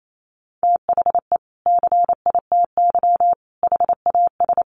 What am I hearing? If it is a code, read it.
Morse code, heard as a keyed tone, about 28 wpm, 700 Hz. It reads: THE CITY HAS
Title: Sentences from Top 1300 Words 28wpm